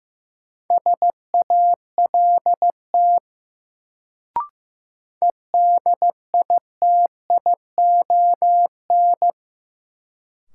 Morse code, heard as a keyed tone, about 15 wpm, 700 Hz. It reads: SALT EDITION
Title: Sets of 5 Words - New Word Drill - 15wpm